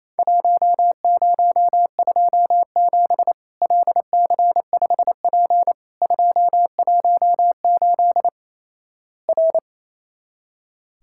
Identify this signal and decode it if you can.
Morse code, heard as a keyed tone, about 28 wpm, 700 Hz. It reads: 1027 LC5P 218 R
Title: Worked All Europe DX Contest - QTC Exchange - 28wpm